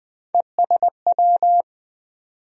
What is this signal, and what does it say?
Morse code, heard as a keyed tone, about 20 wpm, 700 Hz. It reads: ESW
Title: USA Call Sign Postfixes - 3 Letters 20wpm